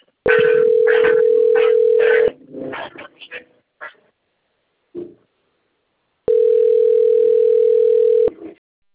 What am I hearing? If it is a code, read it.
Ringback tone (North America), 440 + 480 Hz.